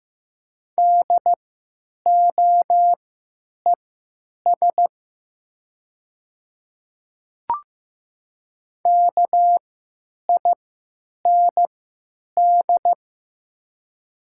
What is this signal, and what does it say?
Morse code, heard as a keyed tone, about 15 wpm, spaced as for 9 wpm, 700 Hz. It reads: DOES KIND
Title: 3x Character Spacing - Top 200 Words - 15wpm